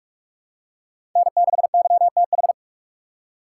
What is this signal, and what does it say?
Morse code, heard as a keyed tone, about 42 wpm, 700 Hz. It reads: N6YTH
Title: USA Call Signs - 3x Character Spacing - 45wpm